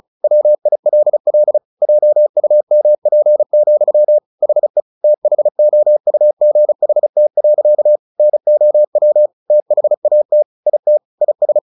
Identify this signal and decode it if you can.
Morse code, heard as a keyed tone, about 34 wpm, 600 Hz. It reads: WILL JUMP, HE THOUGHT. NOW THAT IT IS
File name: the old man and the sea / the old man and the sea 1 35wpm